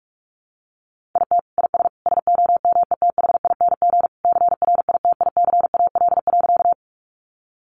Morse code, heard as a keyed tone, about 44 wpm, 700 Hz. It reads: IT IS SOMETHING CRITICAL.
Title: Sentences from Top 1100 Words 45wpm